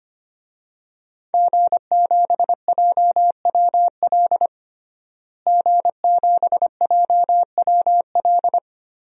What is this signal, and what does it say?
Morse code, heard as a keyed tone, about 25 wpm, 700 Hz. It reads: G7JWL G7JWL
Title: DXCC-100 Call Signs - 5 Characters - Repeated 3x - 25wpm